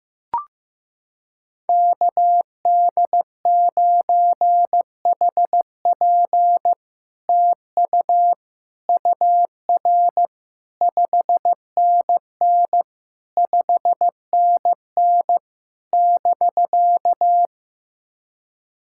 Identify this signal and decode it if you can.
Morse code, heard as a keyed tone, about 15 wpm, 700 Hz. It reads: KD9HP TU UR 5NN 5NN <BK>
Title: SOTA QSO Practice 15wpm